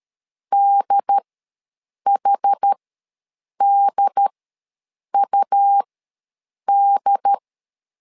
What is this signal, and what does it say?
Morse code, heard as a keyed tone, about 13 wpm, spaced as for 7 wpm, 800 Hz. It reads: DHDUD
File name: LEZIONI / 2 UHDN / 2 UHD ws 12wpm 1722